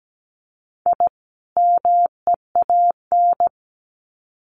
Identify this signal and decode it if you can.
Morse code, heard as a keyed tone, about 17 wpm, 700 Hz. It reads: I MEAN
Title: Sets of 2 Words 17wpm